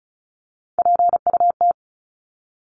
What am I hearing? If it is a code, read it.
Morse code, heard as a keyed tone, about 34 wpm, 700 Hz. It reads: PUT
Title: Top 300 Words 35wpm